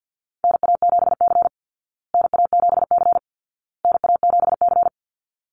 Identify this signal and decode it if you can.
Morse code, heard as a keyed tone, about 50 wpm, 700 Hz. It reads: NA7C NA7C NA7C
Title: Call Signs Repeated 2x - 50wpm